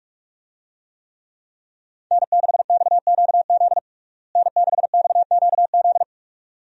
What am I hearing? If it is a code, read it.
Morse code, heard as a keyed tone, about 42 wpm, 700 Hz. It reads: N6XQZ N6XQZ
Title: USA Call Signs - 5 Characters - Repeated 2x - 45wpm